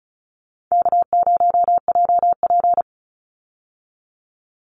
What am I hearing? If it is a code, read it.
Morse code, heard as a keyed tone, about 34 wpm, 700 Hz. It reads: K0JP
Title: Call Signs Repeated 2x - 35wpm